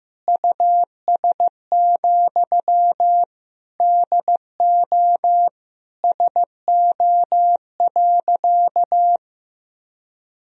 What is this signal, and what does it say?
Morse code, heard as a keyed tone, about 15 wpm, 700 Hz. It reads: US, DO SO.
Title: Sentences from Top 1000 Words - Review Part 2 - 15wpm